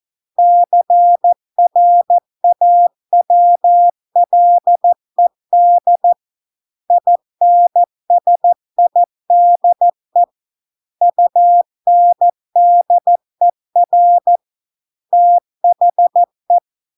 Morse code, heard as a keyed tone, about 14 wpm, 700 Hz. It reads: CRAWLED INSIDE UNDER THE